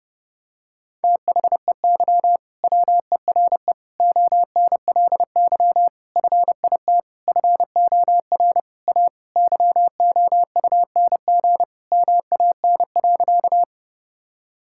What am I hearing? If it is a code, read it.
Morse code, heard as a keyed tone, about 30 wpm, 700 Hz. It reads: THEY WERE ONLY FIT FOR A YOUNG MAN.
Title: Sentences from Top 1000 Words - Review Part 5 - 30wpm